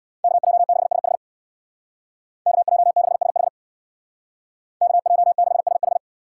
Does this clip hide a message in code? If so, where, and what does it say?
Yes: Morse code from 0.2 to 6.0 s, heard as a keyed tone, about 64 wpm, 700 Hz. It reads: KJ6SH KJ6SH KJ6SH